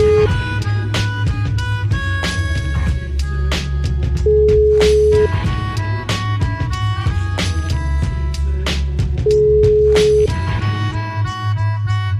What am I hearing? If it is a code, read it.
Ringback tone (Europe, 425 Hz).